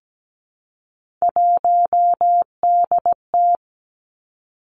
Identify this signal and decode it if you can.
Morse code, heard as a keyed tone, about 17 wpm, 700 Hz. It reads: 1DT
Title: Three Challenging Letter-Number Combinations 17wpm